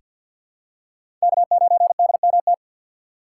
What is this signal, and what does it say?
Morse code, heard as a keyed tone, about 48 wpm, 700 Hz. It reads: K9DMT
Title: USA Call Signs - 5 Characters - 3x Character Spacing - 50wpm